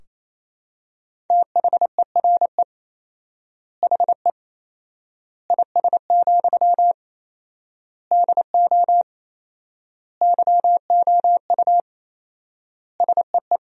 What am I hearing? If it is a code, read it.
Morse code, heard as a keyed tone, about 28 wpm, 700 Hz. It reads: THERE HE IS, DO YOU SEE
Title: 4x Word-Spacing - Sentences from Top 100 Words - 28wpm